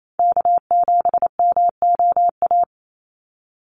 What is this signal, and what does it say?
Morse code, heard as a keyed tone, about 28 wpm, 700 Hz. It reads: K7MOA
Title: Call Signs - 28wpm